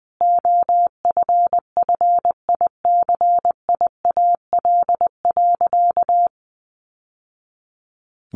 Morse code, heard as a keyed tone, about 20 wpm, 700 Hz. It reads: OFFICIAL.
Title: Sentences from Top 1000 Words 20wpm